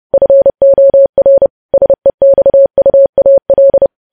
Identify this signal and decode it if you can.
Morse code, heard as a keyed tone, about 30 wpm, 550 Hz. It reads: FOR SEXUAL
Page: News Headlines in Morse Code at 30 WPM